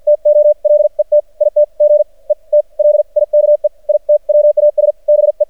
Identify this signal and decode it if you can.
Morse code, heard as a keyed tone, about 48 wpm, spaced as for 39 wpm, 600 Hz. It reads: TO LET IT GET LIKE IT WAS BE
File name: archived / CRSnet